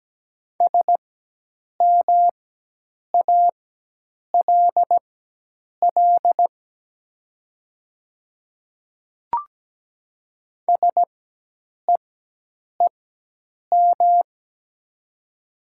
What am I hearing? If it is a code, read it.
Morse code, heard as a keyed tone, about 17 wpm, spaced as for 8 wpm, 700 Hz. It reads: SMALL SEEM
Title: 4x Character Spacing - Top 300 Words - 17wpm